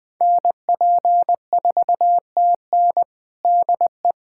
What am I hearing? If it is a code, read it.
Morse code, heard as a keyed tone, about 20 wpm, 700 Hz. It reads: NP4TN DE
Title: USA Callsign DE Callsign 20wpm